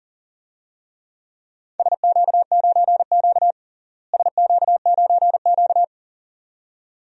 Morse code, heard as a keyed tone, about 40 wpm, 700 Hz. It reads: SQ9Q SQ9Q
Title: DXCC-100 Call Signs - 4 Characters - Repeated 2x - 40wpm